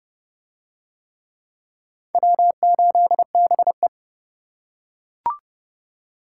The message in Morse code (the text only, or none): W8BE